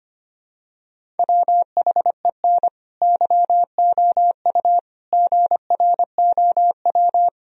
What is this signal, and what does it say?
Morse code, heard as a keyed tone, about 25 wpm, 700 Hz. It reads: WHEN YOU GROW
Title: Sentences from Top 1000 Words - Review Part 1 - 25wpm